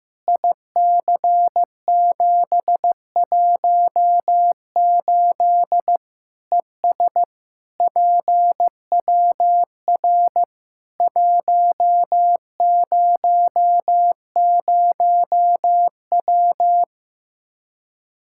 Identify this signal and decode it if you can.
Morse code, heard as a keyed tone, about 15 wpm, 700 Hz. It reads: IC718 ES PWR 100W
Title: Practice QSO - Exchange 2 - Info - 15wpm